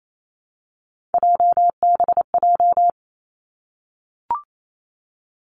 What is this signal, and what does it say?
Morse code, heard as a keyed tone, about 28 wpm, 700 Hz. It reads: JBJ
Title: Three Challenging Letter-Number Combinations 28wpm